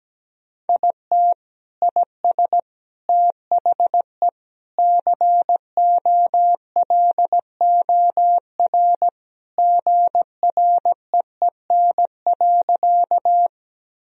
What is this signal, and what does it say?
Morse code, heard as a keyed tone, about 17 wpm, 700 Hz. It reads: IT IS THE COLOR GREEN.
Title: Sentences from Top 1000 Words - Review Part 2 - 17wpm